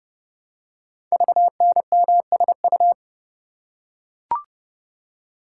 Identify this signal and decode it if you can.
Morse code, heard as a keyed tone, about 30 wpm, 700 Hz. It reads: VNMSU
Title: Code Groups - 5 Letters - 30wpm